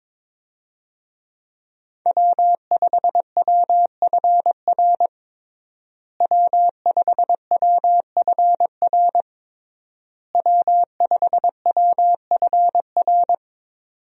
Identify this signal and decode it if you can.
Morse code, heard as a keyed tone, about 22 wpm, 700 Hz. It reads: W5WFR W5WFR W5WFR
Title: USA Call Signs - 5 Characters - Repeated 3x - 22wpm